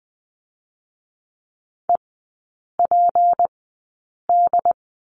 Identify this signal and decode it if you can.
Morse code, heard as a keyed tone, about 20 wpm, 700 Hz. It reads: E P D
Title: 198 Lesson 32e - Letter k - Three Characters 20wpm